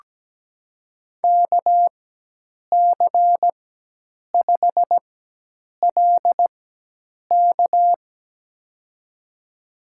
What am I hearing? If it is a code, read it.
Morse code, heard as a keyed tone, about 17 wpm, spaced as for 8 wpm, 700 Hz. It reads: KC5LK
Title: USA Call Signs - 4x Character Spacing - 17wpm